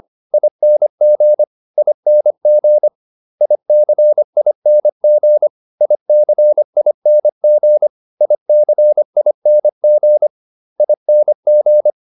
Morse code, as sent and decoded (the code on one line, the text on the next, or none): .. -. --. / .. -. --. / .. -.-. .. -. --. / .. -.-. .. -. --. / .. -.-. .. -. --. / .. -. --.
ING ING ICING ICING ICING ING